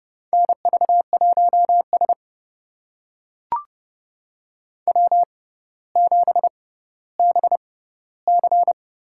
Morse code, heard as a keyed tone, about 31 wpm, spaced as for 16 wpm, 700 Hz. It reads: NV1S W7BC
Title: USA Call Signs - 4 Characters - 6x Character Spacing - 30wpm